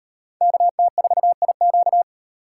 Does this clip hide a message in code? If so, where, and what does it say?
Yes: Morse code from 0.4 to 2.0 s, heard as a keyed tone, about 38 wpm, 700 Hz. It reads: KT4IQ